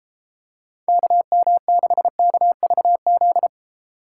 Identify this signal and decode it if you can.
Morse code, heard as a keyed tone, about 29 wpm, 700 Hz. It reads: KM6KVZ